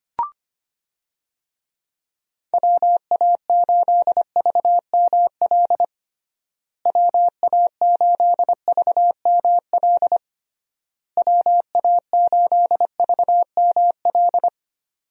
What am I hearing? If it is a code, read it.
Morse code, heard as a keyed tone, about 25 wpm, 700 Hz. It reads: WA8VML WA8VML WA8VML
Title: USA Call Signs - 6 Characters - Repeated 3x - 25wpm